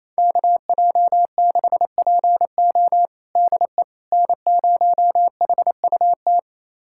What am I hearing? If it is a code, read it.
Morse code, heard as a keyed tone, about 28 wpm, 700 Hz. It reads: KJ6PO DE N0HUT
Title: USA Callsign DE Callsign - 5 Characters - 28wpm